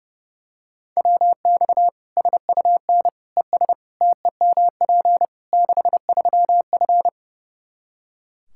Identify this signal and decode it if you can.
Morse code, heard as a keyed tone, about 30 wpm, 700 Hz. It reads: WX SUN ES TEMP 63F
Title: Practice QSO - Exchange 2 - Info - 30wpm